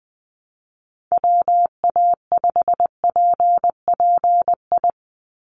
Morse code, heard as a keyed tone, about 20 wpm, 700 Hz. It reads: WA5PPI
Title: USA Call Signs 20wpm